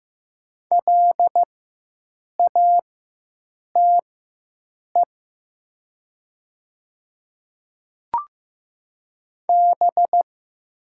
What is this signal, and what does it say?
Morse code, heard as a keyed tone, about 15 wpm, spaced as for 7 wpm, 700 Hz. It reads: LATE B
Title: 4x Character Spacing - Top 300 Words - 15wpm